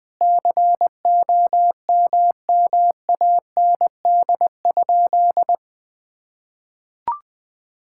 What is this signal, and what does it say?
Morse code, heard as a keyed tone, about 21 wpm, 700 Hz. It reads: COMMAND?
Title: Sentences from Top 1600 Words 20wpm